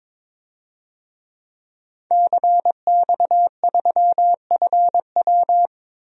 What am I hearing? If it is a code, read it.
Morse code, heard as a keyed tone, about 21 wpm, 700 Hz. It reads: CX3FW
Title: DXCC-100 Call Signs - 5 Characters - 22wpm